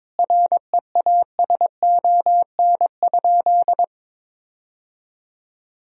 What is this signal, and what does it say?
Morse code, heard as a keyed tone, about 22 wpm, 700 Hz. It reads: REASON?
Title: Sentences from Top 1000 Words - Review Part 1 - 22wpm